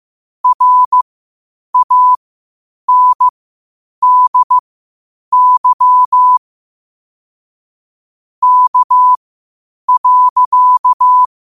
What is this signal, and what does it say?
Morse code, heard as a keyed tone, about 15 wpm, spaced as for 8 wpm, 1000 Hz. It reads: RANDY K.
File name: qotdfiles / 8wpm